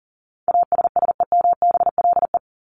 Morse code, heard as a keyed tone, about 40 wpm, 700 Hz. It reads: ASSEMBLE
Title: LY Suffix - Top 100 Words with Base Words 40wpm